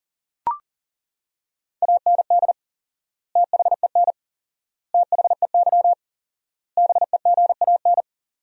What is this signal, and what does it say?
Morse code, heard as a keyed tone, about 40 wpm, 700 Hz. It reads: AND THEN THEY BEGAN
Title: 4x Word-Spacing - Sentences from Top 300 Words - 40wpm